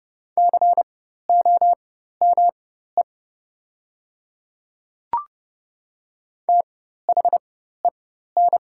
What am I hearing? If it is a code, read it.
Morse code, heard as a keyed tone, about 30 wpm, spaced as for 14 wpm, 700 Hz. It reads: COME THEN
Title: 4x Character Spacing - Top 200 Words - 30wpm